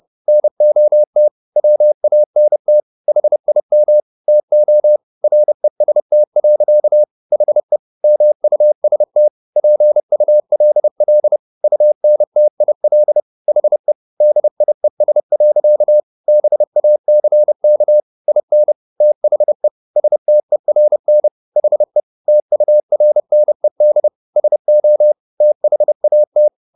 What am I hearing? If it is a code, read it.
Morse code, heard as a keyed tone, about 30 wpm, 600 Hz. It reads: NOT WANT HIM TO REST. HE MUST PULL UNTIL HE DIES. BACK IN THE STERN HE TURNED SO THAT